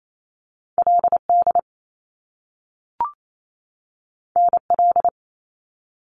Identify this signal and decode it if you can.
Morse code, heard as a keyed tone, about 28 wpm, 700 Hz. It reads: LD NL
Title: Top Two Letter Combinations in Words 28wpm